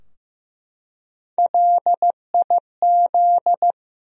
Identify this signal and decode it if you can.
Morse code, heard as a keyed tone, about 15 wpm, 700 Hz. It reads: LIZ
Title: USA Common Names - Max 3 Letters - 15wpm